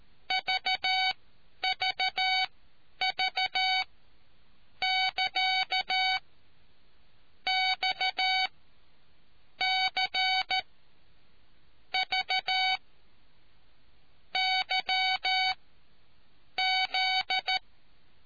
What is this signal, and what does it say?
Morse code, heard as a keyed tone, about 13 wpm, spaced as for 7 wpm, 750 Hz. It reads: VVV<KA> XC V YZ